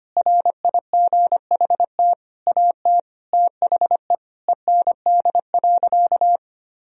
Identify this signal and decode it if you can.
Morse code, heard as a keyed tone, about 25 wpm, 700 Hz. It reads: RIGHT AT THE END.
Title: Sentences from Top 1000 Words - Review Part 2 - 25wpm